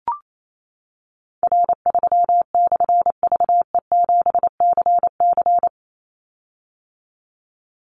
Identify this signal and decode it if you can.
Morse code, heard as a keyed tone, about 28 wpm, 700 Hz. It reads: R3/VE7CC
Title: Call Signs - 28wpm